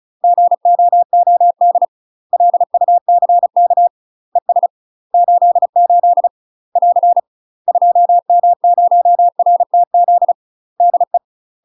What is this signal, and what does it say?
Morse code, heard as a keyed tone, about 34 wpm, 700 Hz. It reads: GOOD LUCK ES 88 <AR> 2M0RTZ DE
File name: UK MASTER QSOs TRAINING / 35WPM / UK MASTER QSOs 0024